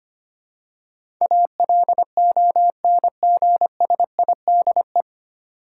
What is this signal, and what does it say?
Morse code, heard as a keyed tone, about 25 wpm, 700 Hz. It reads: ALONGSIDE
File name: Nine Letter Words - 25wpm